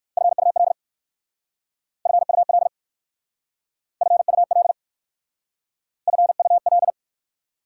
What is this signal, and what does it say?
Morse code, heard as a keyed tone, about 54 wpm, 700 Hz. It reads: FUL FUL FUL FUL
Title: Speed-Racing Prefixes and Suffixes 45wpm